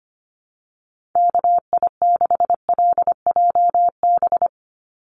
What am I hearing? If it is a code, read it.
Morse code, heard as a keyed tone, about 25 wpm, 700 Hz. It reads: KI6LJB
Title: USA Call Signs 25wpm